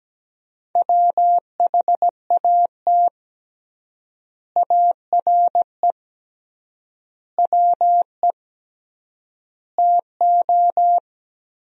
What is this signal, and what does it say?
Morse code, heard as a keyed tone, about 17 wpm, 700 Hz. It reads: WHAT ARE WE TO
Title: 3x Word-Spacing - Sentences from Top 100 Words - 17wpm